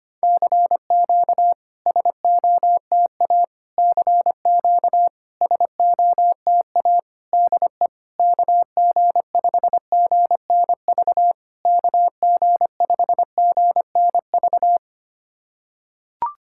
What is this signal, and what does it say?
Morse code, heard as a keyed tone, about 25 wpm, 700 Hz. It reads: CQ SOTA CQ SOTA DE KG5GNV KG5GNV
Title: SOTA QSO Practice 25wpm